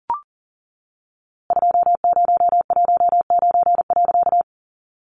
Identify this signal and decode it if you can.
Morse code, heard as a keyed tone, about 38 wpm, 700 Hz. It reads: 2019.
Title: A Sound Of Thunder 40wpm